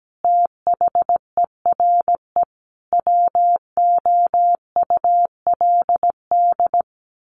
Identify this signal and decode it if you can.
Morse code, heard as a keyed tone, about 17 wpm, 700 Hz. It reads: THERE WOULD
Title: Sentences from Top 1200 Words 17wpm